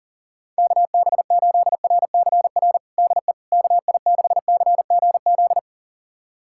Morse code, heard as a keyed tone, about 37 wpm, 690 Hz. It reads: KB8RCR DE KI6CGZ
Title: USA Callsign DE Callsign - 6 Characters - 40wpm